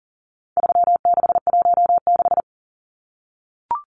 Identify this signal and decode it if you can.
Morse code, heard as a keyed tone, about 39 wpm, 700 Hz. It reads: 3616
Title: Numbers - 4-Digit 40wpm